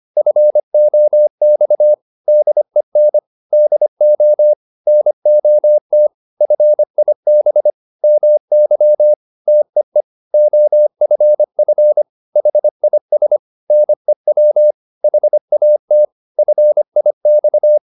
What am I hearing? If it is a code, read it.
Morse code, heard as a keyed tone, about 25 wpm, 600 Hz. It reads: FOX DEN DO NOT FIB MY TEE OFF HIS NEW HAT FIX